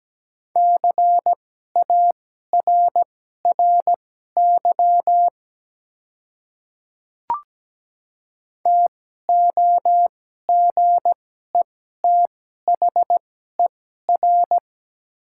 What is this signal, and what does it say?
Morse code, heard as a keyed tone, about 17 wpm, spaced as for 12 wpm, 690 Hz. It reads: CARRY TOGETHER
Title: 2x Character Spacing - Top 300 Words - 17wpm